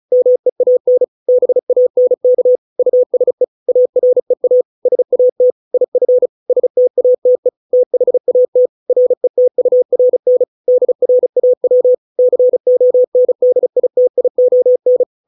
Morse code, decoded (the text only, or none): MEAN BANK USE AREA SAT IF STATE THAT RETURN DRAW CONDITION